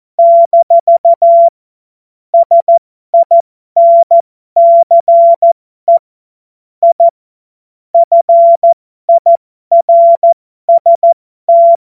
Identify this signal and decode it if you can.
Morse code, heard as a keyed tone, about 14 wpm, spaced as for 12 wpm, 680 Hz. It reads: - SINCE I FIRST
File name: Morsekurs / Witze